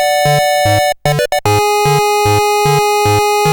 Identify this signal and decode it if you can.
There is no code.